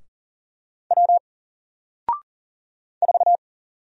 Morse code, heard as a keyed tone, about 40 wpm, 700 Hz. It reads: W 4
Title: Single Character (Letter:Number:Punctuation:Prosign) 40wpm